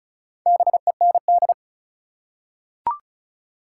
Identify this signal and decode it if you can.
Morse code, heard as a keyed tone, about 34 wpm, 700 Hz. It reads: BEND
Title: Four Letter Words - 35wpm